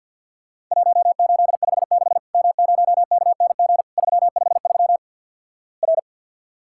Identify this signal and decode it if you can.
Morse code, heard as a keyed tone, about 48 wpm, 700 Hz. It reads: 1856 M0KNG 353 R
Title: Worked All Europe DX Contest - QTC Exchange - 50wpm